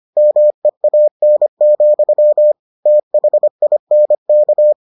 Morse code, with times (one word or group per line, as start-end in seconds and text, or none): 0.2-2.5 MEAN,
2.9-4.7 THINK